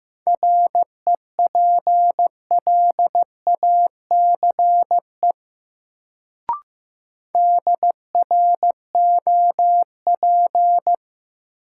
Morse code, heard as a keyed tone, about 15 wpm, 700 Hz. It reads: REPLACE DROP
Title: Sentences from Top 1400 Words - New Word Drill - 15wpm